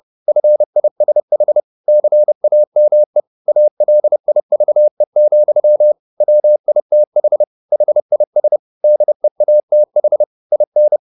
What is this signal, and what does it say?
Morse code, heard as a keyed tone, about 30 wpm, 600 Hz. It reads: FISH CAME ALIVE, WITH HIS DEATH IN